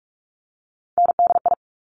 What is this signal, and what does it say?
Morse code, heard as a keyed tone, about 46 wpm, 690 Hz. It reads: NDI
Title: Top Three Letter Combinations in Words 45wpm